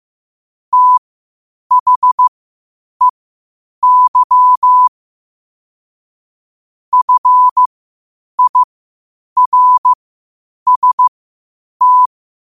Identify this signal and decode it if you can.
Morse code, heard as a keyed tone, about 15 wpm, spaced as for 8 wpm, 1000 Hz. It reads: THEY FIRST
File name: qotdfiles / 8wpm